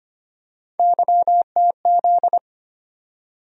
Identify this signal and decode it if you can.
Morse code, heard as a keyed tone, about 25 wpm, 700 Hz. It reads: YTZ
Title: USA Call Sign Postfixes - 3 Letters 25wpm